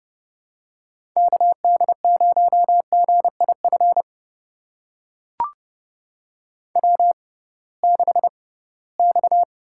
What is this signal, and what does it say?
Morse code, heard as a keyed tone, about 32 wpm, spaced as for 20 wpm, 700 Hz. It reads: KD0GIF W6X
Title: USA Call Signs - 6x Character Spacing - 30wpm